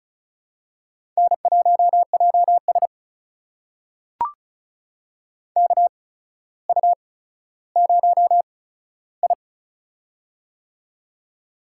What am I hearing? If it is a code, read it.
Morse code, heard as a keyed tone, about 33 wpm, spaced as for 15 wpm, 700 Hz. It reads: N1JS KU0I
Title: USA Call Signs - 4 Characters - 8x Character Spacing - 35wpm